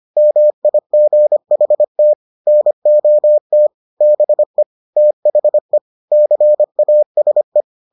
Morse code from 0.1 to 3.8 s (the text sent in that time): MIGHT NOT